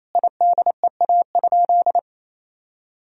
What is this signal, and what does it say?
Morse code, heard as a keyed tone, about 28 wpm, 700 Hz. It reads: IDEA?
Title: Sentences from Top 1000 Words - Review Part 1 - 28wpm